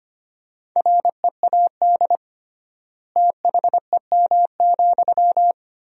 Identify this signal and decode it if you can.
Morse code, heard as a keyed tone, about 25 wpm, 700 Hz. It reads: READ THEM,
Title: 3x Word-Spacing - Sentences from Top 400 Words - 25wpm